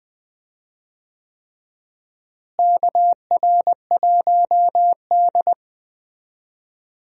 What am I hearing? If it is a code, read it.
Morse code, heard as a keyed tone, about 20 wpm, 700 Hz. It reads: KR1D
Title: USA Call Signs - 4 Characters - Repeated 3x - 20wpm